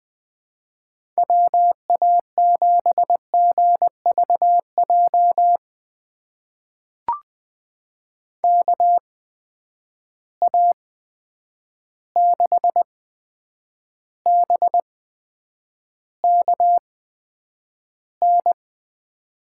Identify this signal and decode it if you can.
Morse code, heard as a keyed tone, about 20 wpm, spaced as for 9 wpm, 700 Hz. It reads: WA7GVJ KA6BKN
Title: USA Call Signs - 6 Characters - 8x Character Spacing - 20wpm